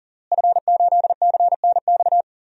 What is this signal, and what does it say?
Morse code, heard as a keyed tone, about 40 wpm, 690 Hz. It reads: F8CNX
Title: DXCC-100 Call Signs - 5 Characters - Repeated 2x - 40wpm